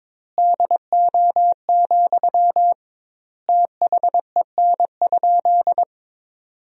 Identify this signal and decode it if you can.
Morse code, heard as a keyed tone, about 21 wpm, 700 Hz. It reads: DO, THEN?
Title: 2x Word-Spacing - Sentences from Top 100 Words - 22wpm